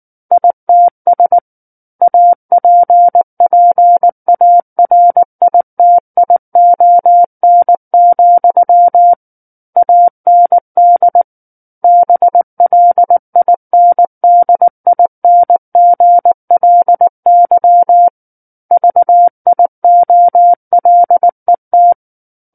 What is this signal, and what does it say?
Morse code, heard as a keyed tone, about 19 wpm, 700 Hz. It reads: ITS APPARITION, AND BLINDINGLY VIOLET
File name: Book One / 10-Chapter 10 - 19 WPM